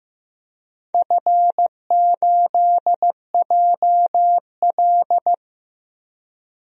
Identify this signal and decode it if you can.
Morse code, heard as a keyed tone, about 15 wpm, 700 Hz. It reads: F8JL
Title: DXCC-100 Call Signs - 4 Characters - 15wpm